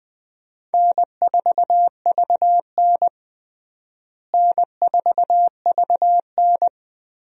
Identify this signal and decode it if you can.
Morse code, heard as a keyed tone, about 20 wpm, 700 Hz. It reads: N4VN N4VN
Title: USA Call Signs - 4 Characters - Repeated 2x - 20wpm